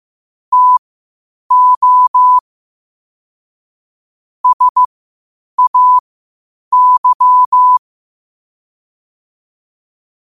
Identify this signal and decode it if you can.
Morse code, heard as a keyed tone, about 15 wpm, spaced as for 8 wpm, 1000 Hz. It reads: TO SAY
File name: qotdfiles / 8wpm